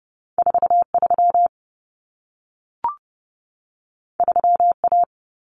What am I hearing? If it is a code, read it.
Morse code, heard as a keyed tone, about 30 wpm, 700 Hz. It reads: 43 3A
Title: Common Cut Numbers - 2-Digit 30wpm